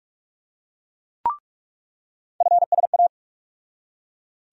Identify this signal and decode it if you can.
Morse code, heard as a keyed tone, about 42 wpm, 700 Hz. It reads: FSA
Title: USA Call Sign Postfixes - 3 Letters 45wpm